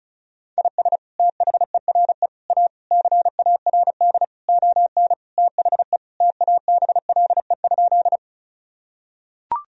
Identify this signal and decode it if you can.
Morse code, heard as a keyed tone, about 36 wpm, 700 Hz. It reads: IS THERE A CARD ON THE TABLE?
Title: Sentences from Top 1000 Words - Review Part 4 - 35wpm